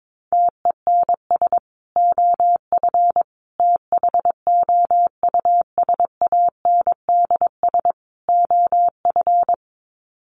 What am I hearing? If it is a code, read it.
Morse code, heard as a keyed tone, about 22 wpm, 700 Hz. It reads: TENS OF THOUSANDS OF
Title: Sets of 4 Words 22wpm